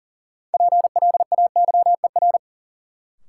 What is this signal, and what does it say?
Morse code, heard as a keyed tone, about 40 wpm, 700 Hz. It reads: PLAYER
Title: Sentences from Top 1000 Words - Review Part 4 - New Word Drill - 40wpm